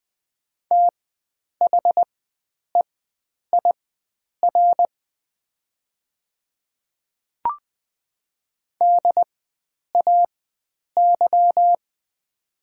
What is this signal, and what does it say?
Morse code, heard as a keyed tone, about 20 wpm, spaced as for 9 wpm, 700 Hz. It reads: THEIR DAY
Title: 4x Character Spacing - Top 100 Words - 20wpm